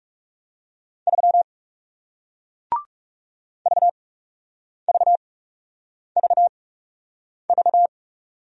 Morse code, heard as a keyed tone, about 40 wpm, spaced as for 8 wpm, 700 Hz. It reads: 3 VVVV